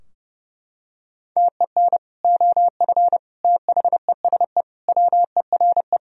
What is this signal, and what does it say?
Morse code, heard as a keyed tone, about 32 wpm, 700 Hz. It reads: TEN OF THESE WERE
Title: Sentences from Top 1000 Words - Review Part 2 - 30wpm